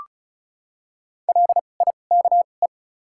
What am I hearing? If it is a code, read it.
Morse code, heard as a keyed tone, about 34 wpm, spaced as for 25 wpm, 700 Hz. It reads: LIKE
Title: 2x Character Spacing - Top 200 Words - 35wpm